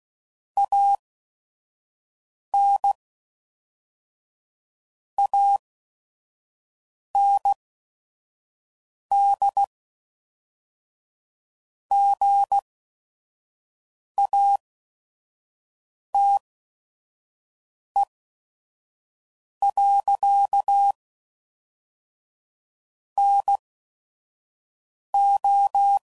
Morse code, heard as a keyed tone, about 16 wpm, spaced as for 5 wpm, 800 Hz. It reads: AN AND GATE. NO
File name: CW Code Groups @5WPM / Technical Subjects / AND Gate